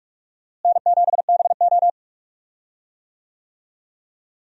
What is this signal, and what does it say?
Morse code, heard as a keyed tone, about 44 wpm, 700 Hz. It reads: N7BO
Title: USA Call Signs - 4 Characters - Repeated 2x - 45wpm